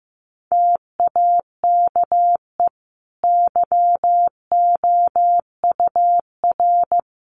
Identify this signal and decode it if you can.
Morse code, heard as a keyed tone, about 14 wpm, 700 Hz. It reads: TAKE YOUR
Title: Sentences from Top 1500 Words 15wpm